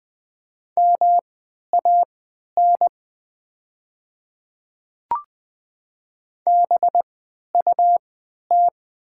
Morse code, heard as a keyed tone, about 20 wpm, spaced as for 11 wpm, 700 Hz. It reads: MAN BUT
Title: 3x Character Spacing - Top 300 Words - 20wpm